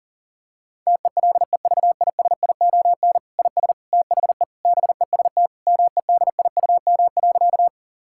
Morse code, heard as a keyed tone, about 40 wpm, 700 Hz. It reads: TELEVISION IS THE BEST MEDIUM.
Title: Sentences from Top 1000 Words - Review Part 6 - 40wpm